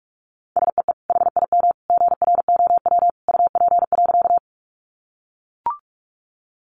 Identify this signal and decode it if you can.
Morse code, heard as a keyed tone, about 44 wpm, 700 Hz. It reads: SEE HIM GROW UP.
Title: Sentences from Top 300 Words 45wpm